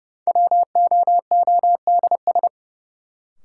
Morse code, heard as a keyed tone, about 30 wpm, 700 Hz. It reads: WOODS
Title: Sentences from Top 1000 Words - Review Part 2 - New Word Drill - 30wpm